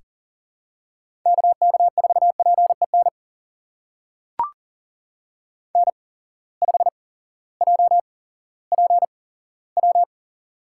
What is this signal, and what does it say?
Morse code, heard as a keyed tone, about 40 wpm, spaced as for 18 wpm, 700 Hz. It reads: KK4PEN N5JPW
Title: USA Call Signs - 8x Character Spacing - 40wpm